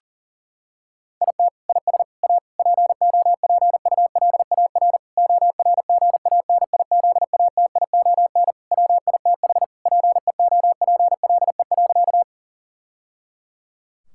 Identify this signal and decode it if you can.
Morse code, heard as a keyed tone, about 40 wpm, 700 Hz. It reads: IT IS A POPULAR ORGANIZATION WITH PEOPLE.
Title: Sentences from Top 1000 Words - Review Part 5 - 40wpm